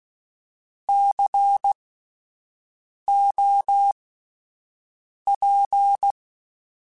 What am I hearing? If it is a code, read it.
Morse code, heard as a keyed tone, about 16 wpm, spaced as for 5 wpm, 800 Hz. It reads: COP